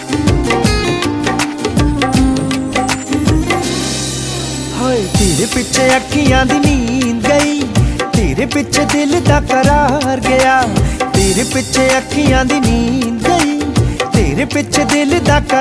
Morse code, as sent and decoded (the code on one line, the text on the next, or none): none